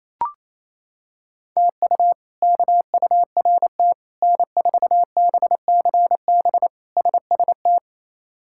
Morse code, heard as a keyed tone, about 28 wpm, 700 Hz. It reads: TU KURT N4BCB SST
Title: K1USN SST QSO Practice - 28wpm